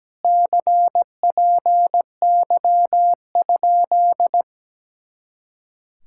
Morse code, heard as a keyed tone, about 17 wpm, 700 Hz. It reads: CPY?
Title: Speed-Racing Common QSO Phrases - 17wpm